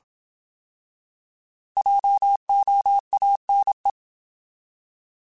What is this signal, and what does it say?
Morse code, heard as a keyed tone, about 27 wpm, 800 Hz. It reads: JOANE
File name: Alphabetical / Alphabetical mp3s / Js @25